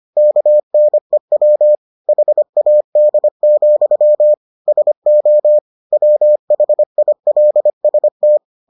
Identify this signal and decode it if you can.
Morse code, heard as a keyed tone, about 25 wpm, 600 Hz. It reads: KNEW HAD, SO WHILST